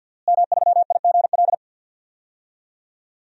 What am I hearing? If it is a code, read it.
Morse code, heard as a keyed tone, about 51 wpm, 700 Hz. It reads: M3IGL
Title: DXCC-100 Call Signs - 5 Characters - Repeated 2x - 50wpm